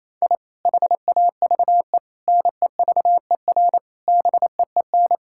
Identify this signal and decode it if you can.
Morse code, heard as a keyed tone, about 28 wpm, 700 Hz. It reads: I HAVE NEVER BEEN
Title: Sentences from Top 1000 Words - Review Part 5 - 28wpm